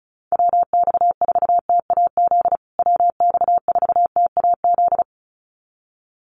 Morse code, heard as a keyed tone, about 34 wpm, 700 Hz. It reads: WX4TAZ WX4TAZ
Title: USA Call Signs Repeated 2x 35wpm